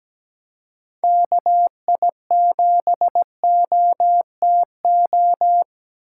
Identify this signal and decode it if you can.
Morse code, heard as a keyed tone, about 17 wpm, 700 Hz. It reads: KI7OTO
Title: USA Call Signs - 6 Characters - 3x Character Spacing - 17wpm